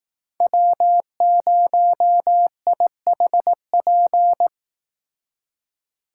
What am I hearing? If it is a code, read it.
Morse code, heard as a keyed tone, about 19 wpm, 700 Hz. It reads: W0IHP